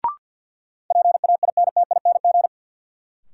Morse code, heard as a keyed tone, about 53 wpm, 700 Hz. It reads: PAINTING
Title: Sentences from Top 1200 Words - New Word Drill Alt - 50wpm